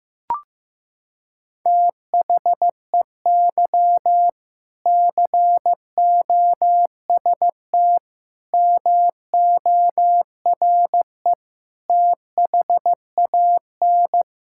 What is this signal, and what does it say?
Morse code, heard as a keyed tone, about 15 wpm, 700 Hz. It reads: THEY COST MORE THAN
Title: Sentences from Top 1000 Words - Review Part 5 - 15wpm